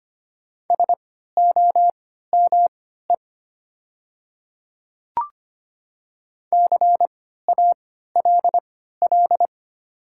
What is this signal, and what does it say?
Morse code, heard as a keyed tone, about 25 wpm, spaced as for 14 wpm, 690 Hz. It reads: SOME CALL
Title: 3x Character Spacing - Top 200 Words - 25wpm